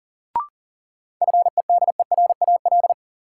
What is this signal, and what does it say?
Morse code, heard as a keyed tone, about 40 wpm, 700 Hz. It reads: FEDERAL
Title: Sets of 3 Words - New Word Drill - 40wpm